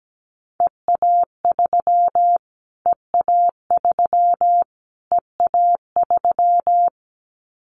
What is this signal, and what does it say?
Morse code, heard as a keyed tone, about 17 wpm, 700 Hz. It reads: EA3 EA3 EA3
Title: Callsign Prefixes for Top 25 DXCC - Repeated 3x - 17wpm